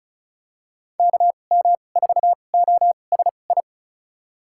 Morse code, heard as a keyed tone, about 36 wpm, spaced as for 25 wpm, 700 Hz. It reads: KM4OSI
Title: USA Call Signs - 6 Characters - 2x Character Spacing - 35wpm